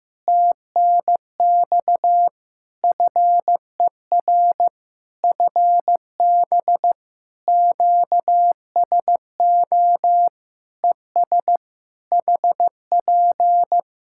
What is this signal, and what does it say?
Morse code, heard as a keyed tone, about 15 wpm, 700 Hz. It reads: TNX FER FB QSO ES HP
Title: Practice QSO - Exchange 4 - Ending - 15wpm